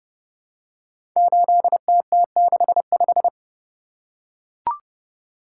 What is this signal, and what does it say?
Morse code, heard as a keyed tone, about 30 wpm, 700 Hz. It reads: 8TT65
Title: Code Groups - 5 Letter:Numbers - 30wpm